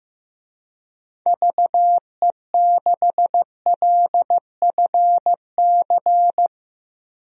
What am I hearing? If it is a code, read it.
Morse code, heard as a keyed tone, about 15 wpm, 700 Hz. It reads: VE6LFC
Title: DXCC-100 Call Signs - 6 Characters - 15wpm